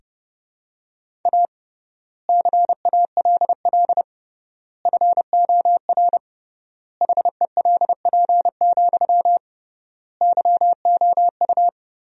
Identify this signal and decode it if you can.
Morse code, heard as a keyed tone, about 30 wpm, 700 Hz. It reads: A CALL FOR HELP, YOU
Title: 3x Word-Spacing - Sentences from Top 300 Words - 30wpm